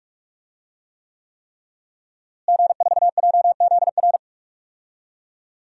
Morse code, heard as a keyed tone, about 46 wpm, 700 Hz. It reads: G4JZR
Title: DXCC-100 Call Signs - 5 Characters - Repeated 3x - 45wpm